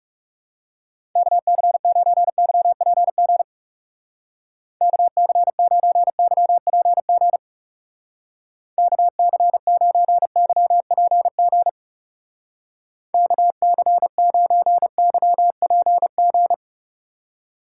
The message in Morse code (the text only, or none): KC9YPG KC9YPG KC9YPG KC9YPG